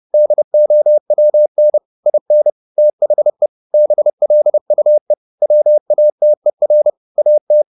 Morse code, heard as a keyed tone, about 30 wpm, 600 Hz. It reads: DOWN IN THE BLUE WATER AT
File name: the old man and the sea / the old man and the sea 1 30wpm